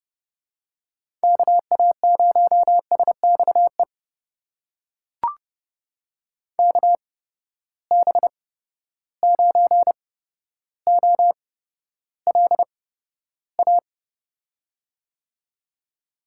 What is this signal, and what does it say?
Morse code, heard as a keyed tone, about 28 wpm, spaced as for 13 wpm, 700 Hz. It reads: KA0SXE KB9OLA
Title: USA Call Signs - 6 Characters - 8x Character Spacing - 30wpm